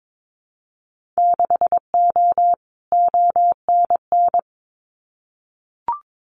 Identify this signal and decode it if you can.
Morse code, heard as a keyed tone, about 22 wpm, 700 Hz. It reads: 6O ONN
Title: Winter Field Day Exchanges Encore 22wpm